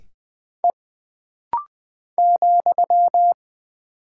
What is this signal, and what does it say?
Morse code, heard as a keyed tone, about 20 wpm, 700 Hz. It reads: E ,
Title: Single Character (Letter:Number:Punctuation:Prosign) - Rapid-Fire - 20wpm